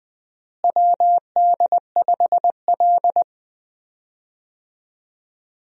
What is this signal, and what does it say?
Morse code, heard as a keyed tone, about 20 wpm, 700 Hz. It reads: WD5L
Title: USA Call Signs - 4 Characters - Repeated 2x - 20wpm